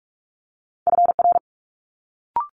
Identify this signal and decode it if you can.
Morse code, heard as a keyed tone, about 46 wpm, 700 Hz. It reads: FR
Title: Multi-letter Phonemes Quick Fire 45wpm